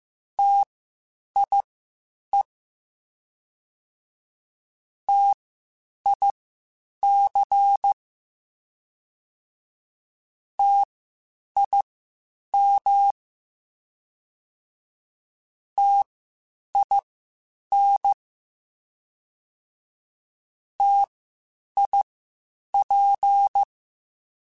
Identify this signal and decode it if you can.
Morse code, heard as a keyed tone, about 15 wpm, spaced as for 8 wpm, 800 Hz. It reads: TIE TIC TIM TIN TIP
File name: Spaced Alphas / Spaced Alphas mp3s / Ts sp @13